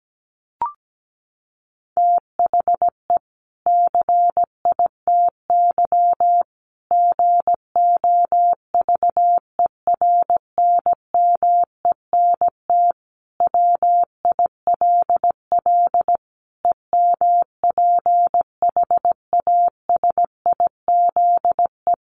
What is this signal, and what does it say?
Morse code, heard as a keyed tone, about 17 wpm, 700 Hz. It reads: THE CITY GOVERNMENT WILL EMPHASIZE
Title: Sentences from Top 1500 Words 17wpm